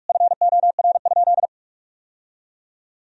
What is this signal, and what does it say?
Morse code, heard as a keyed tone, about 46 wpm, 700 Hz. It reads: FOR?
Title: Sentences from Top 1000 Words - Review Part 4 - 45wpm